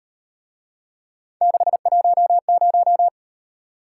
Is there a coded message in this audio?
Yes: Morse code from 1.4 to 3.1 s, heard as a keyed tone, about 38 wpm, 700 Hz. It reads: 610